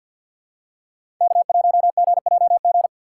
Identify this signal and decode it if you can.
Morse code, heard as a keyed tone, about 48 wpm, 700 Hz. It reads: K1GJG
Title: USA Call Signs - 5 Characters - 3x Character Spacing - 50wpm